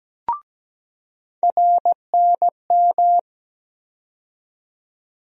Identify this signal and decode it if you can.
Morse code, heard as a keyed tone, about 17 wpm, 700 Hz. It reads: RNM
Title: USA Call Sign Postfixes - 3 Letters 17wpm